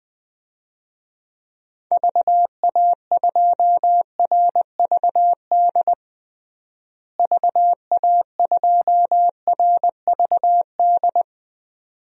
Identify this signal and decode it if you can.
Morse code, heard as a keyed tone, about 21 wpm, 700 Hz. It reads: VA2RVD VA2RVD
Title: DXCC-100 Call Signs - 6 Characters - Repeated 2x - 20wpm